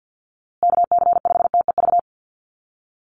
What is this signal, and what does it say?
Morse code, heard as a keyed tone, about 50 wpm, 700 Hz. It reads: KC5TEV
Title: USA Call Signs Repeated 2x 50wpm